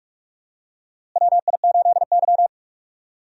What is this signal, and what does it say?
Morse code, heard as a keyed tone, about 46 wpm, 700 Hz. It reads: WI8Y
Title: USA Call Signs - 4 Characters - Repeated 3x - 45wpm